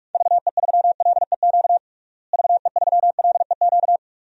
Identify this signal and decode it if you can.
Morse code, heard as a keyed tone, about 46 wpm, 700 Hz. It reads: VE3LEQ VE3LEQ
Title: DXCC-100 Call Signs - 6 Characters - Repeated 3x - 45wpm